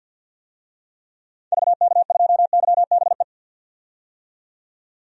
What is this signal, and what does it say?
Morse code, heard as a keyed tone, about 47 wpm, 700 Hz. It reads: VK2YBE
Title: DXCC-100 Call Signs - 6 Characters - 50wpm